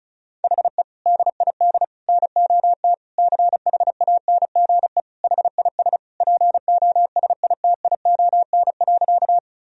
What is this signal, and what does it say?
Morse code, heard as a keyed tone, about 36 wpm, 700 Hz. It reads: HE DID NOT CHANGE HIS POSITION.
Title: Sentences from Top 1000 Words - Review Part 1 - 35wpm